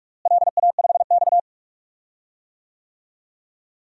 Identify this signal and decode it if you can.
Morse code, heard as a keyed tone, about 43 wpm, 700 Hz. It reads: LA5X
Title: DXCC-100 Call Signs - 4 Characters - Repeated 3x - 45wpm